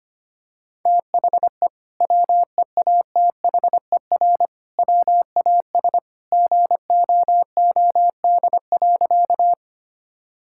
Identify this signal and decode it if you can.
Morse code, heard as a keyed tone, about 25 wpm, 700 Hz. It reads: THE WEATHER WAS GOOD.
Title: Sentences from Top 1000 Words - Review Part 5 - 25wpm